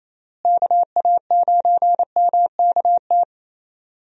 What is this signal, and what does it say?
Morse code, heard as a keyed tone, about 28 wpm, 700 Hz. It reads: KA9MKT
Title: USA Call Signs - 6 Characters - Repeated 2x - 28wpm